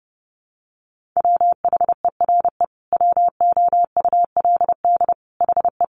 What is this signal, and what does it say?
Morse code, heard as a keyed tone, about 30 wpm, 700 Hz. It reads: WHERE WOULD HE
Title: Sentences from Top 1300 Words 30wpm